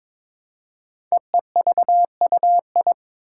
Morse code, heard as a keyed tone, about 22 wpm, 700 Hz. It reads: EEVUI
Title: Code Groups - 5 Letters - 22wpm